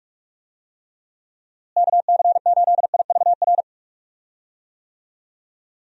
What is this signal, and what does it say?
Morse code, heard as a keyed tone, about 41 wpm, 700 Hz. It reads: KC8IVR
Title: USA Call Signs - 6 Characters - Repeated 2x - 45wpm